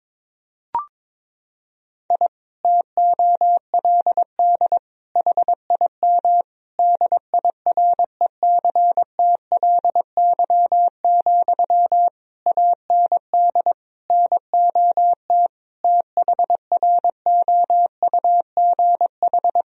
Morse code, heard as a keyed tone, about 22 wpm, 700 Hz. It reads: I TOLD HIM DIRECTLY, AND NOT THROUGH
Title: Sentences from Top 1000 Words - Review Part 1 - 22wpm